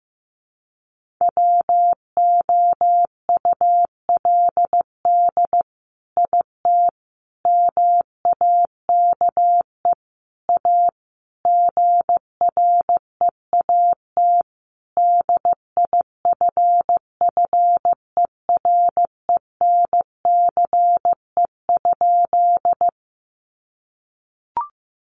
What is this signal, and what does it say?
Morse code, heard as a keyed tone, about 15 wpm, 700 Hz. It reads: WOULD IT MAKE A GREAT DIFFERENCE?
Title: Sentences from Top 800 Words 15wpm